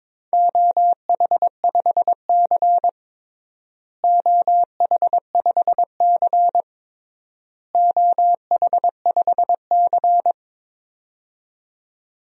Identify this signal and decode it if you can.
Morse code, heard as a keyed tone, about 22 wpm, 700 Hz. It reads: OH5C OH5C OH5C
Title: DXCC-100 Call Signs - 4 Characters - Repeated 3x - 22wpm